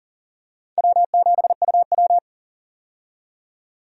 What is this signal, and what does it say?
Morse code, heard as a keyed tone, about 40 wpm, 700 Hz. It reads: W7UW
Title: USA Call Signs - 4 Characters - 40wpm